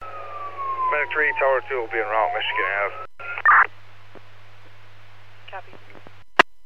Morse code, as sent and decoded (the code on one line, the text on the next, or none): none